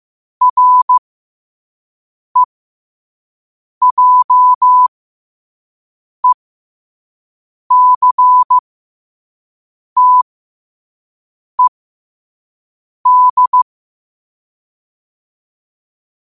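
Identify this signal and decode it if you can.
Morse code, heard as a keyed tone, about 15 wpm, spaced as for 5 wpm, 1000 Hz. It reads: REJECTED
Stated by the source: Morse code at 5 wpm.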